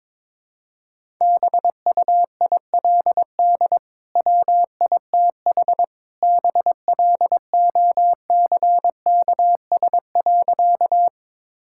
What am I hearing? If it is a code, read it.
Morse code, heard as a keyed tone, about 22 wpm, 700 Hz. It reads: BUILD WITH BLOCKS.
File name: practice / 22wpm / Easy Sentences with Hints - 22wpm